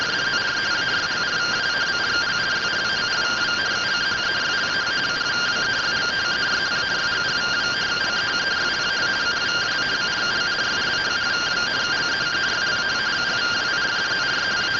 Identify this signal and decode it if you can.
Two-tone teleprinter signal, about 170 Hz shift.